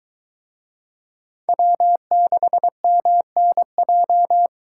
Morse code, heard as a keyed tone, about 22 wpm, 700 Hz. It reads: W6MNJ